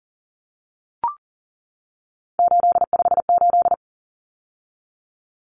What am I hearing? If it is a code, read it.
Morse code, heard as a keyed tone, about 40 wpm, 700 Hz. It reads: 858
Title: Numbers - 3-Digit 40wpm